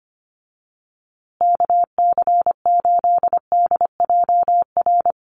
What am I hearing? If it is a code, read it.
Morse code, heard as a keyed tone, about 25 wpm, 700 Hz. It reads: KC8DJR
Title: USA Call Signs Repeated 2x 25wpm